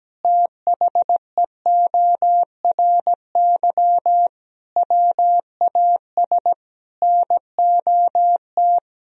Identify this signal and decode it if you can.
Morse code, heard as a keyed tone, about 17 wpm, 700 Hz. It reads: THEORY WAS NOT
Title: Sentences from Top 1000 Words - Review Part 3 - 17wpm